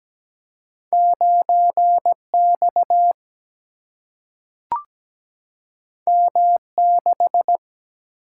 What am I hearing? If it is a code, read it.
Morse code, heard as a keyed tone, about 17 wpm, 700 Hz. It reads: 9X M6
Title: Single-Letter Single-Number 17wpm